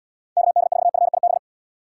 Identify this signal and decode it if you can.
Morse code, heard as a keyed tone, about 79 wpm, 700 Hz. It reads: KD4FIB